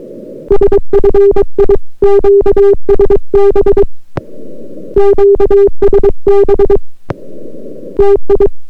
Morse code, heard as a keyed tone, about 22 wpm, 400 Hz. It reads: SFI QSB QSB TI